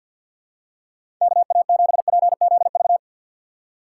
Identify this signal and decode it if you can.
Morse code, heard as a keyed tone, about 54 wpm, 700 Hz. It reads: KA7PZV
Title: USA Call Signs - 6 Characters - 6x Character Spacing - 50wpm